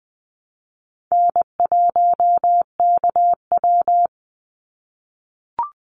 Morse code, heard as a keyed tone, about 20 wpm, 700 Hz. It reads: N1KW
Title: Easy Call Signs - 20wpm